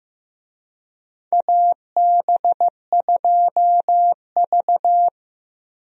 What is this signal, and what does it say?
Morse code, heard as a keyed tone, about 15 wpm, 700 Hz. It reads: AB2V
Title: USA Call Signs - 4 Characters - Repeated 2x - 15wpm